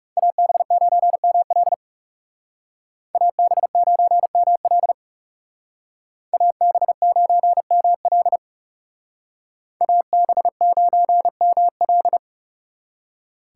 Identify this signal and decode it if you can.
Morse code, heard as a keyed tone, about 36 wpm, 700 Hz. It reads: AB9ML AB9ML AB9ML AB9ML
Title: Speed Racing - USA Call Signs 30wpm